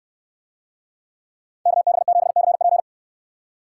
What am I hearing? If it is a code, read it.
Morse code, heard as a keyed tone, about 63 wpm, 690 Hz. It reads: KB7CP